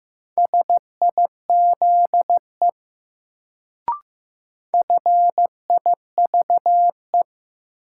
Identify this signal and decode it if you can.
Morse code, heard as a keyed tone, about 15 wpm, 700 Hz. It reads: SIZE FIVE
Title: Sentences from Top 400 Words - New Word Drill - 15wpm